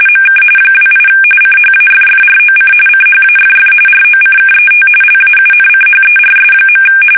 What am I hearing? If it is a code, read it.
RTTY, about 45.45 baud. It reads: MK 28 CHAFFROC CHAFF ROCKET LAUNCHERS.
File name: fsk850 / battleshipnj